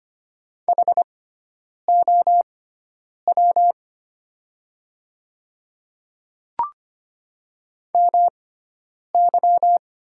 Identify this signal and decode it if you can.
Morse code, heard as a keyed tone, about 25 wpm, spaced as for 9 wpm, 700 Hz. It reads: HOW MY
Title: 6x Character Spacing - Top 100 Words - 25wpm